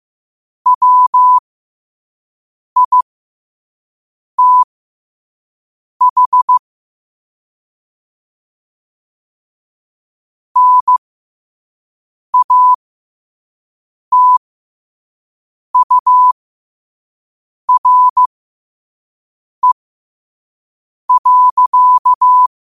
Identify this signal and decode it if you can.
Morse code, heard as a keyed tone, about 15 wpm, spaced as for 5 wpm, 1000 Hz. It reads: WITH NATURE.
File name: qotdfiles / 5wpm